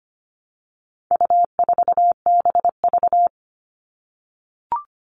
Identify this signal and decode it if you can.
Morse code, heard as a keyed tone, about 25 wpm, 700 Hz. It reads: U4BV
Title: All Cut Numbers - 4-Digit 25wpm